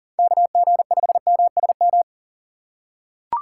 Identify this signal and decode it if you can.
Morse code, heard as a keyed tone, about 40 wpm, 700 Hz. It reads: KG5MSM
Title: USA Call Signs - 2x Character Spacing - 40wpm